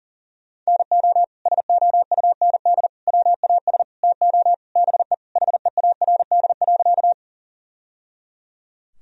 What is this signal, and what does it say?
Morse code, heard as a keyed tone, about 40 wpm, 700 Hz. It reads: NO SOUND WAS TO BE HEARD.
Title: Sentences from Top 1000 Words - Review Part 1 - 40wpm